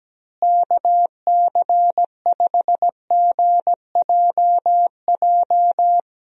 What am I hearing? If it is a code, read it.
Morse code, heard as a keyed tone, about 17 wpm, 700 Hz. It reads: KC5GJJ
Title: USA Call Signs - 6 Characters - Repeated 2x - 17wpm